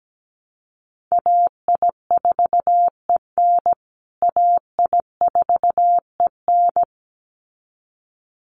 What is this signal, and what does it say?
Morse code, heard as a keyed tone, about 17 wpm, 700 Hz. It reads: AI4EN AI4EN
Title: USA Call Signs Repeated 2x 17wpm